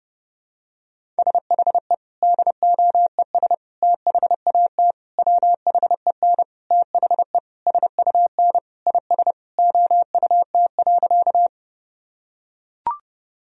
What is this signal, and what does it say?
Morse code, heard as a keyed tone, about 30 wpm, 700 Hz. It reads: SHE DOES THAT WHEN THE SUN IS OUT.
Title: Sentences from Top 1000 Words - Review Part 1 - 30wpm